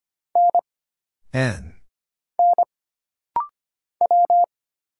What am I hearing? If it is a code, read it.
Morse code, heard as a keyed tone, about 25 wpm, 700 Hz. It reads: N N W
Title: Single Letters - Rapid-Fire 25wpm